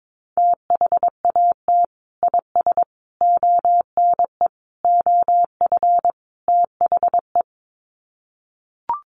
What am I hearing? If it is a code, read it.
Morse code, heard as a keyed tone, about 22 wpm, 700 Hz. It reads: THAT IS ONE OF THE
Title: Sets of 5 Words Encore 22wpm